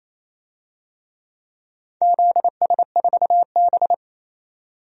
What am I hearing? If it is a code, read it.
Morse code, heard as a keyed tone, about 28 wpm, 690 Hz. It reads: ZS4B
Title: DXCC-100 Call Signs - 4 Characters - 28wpm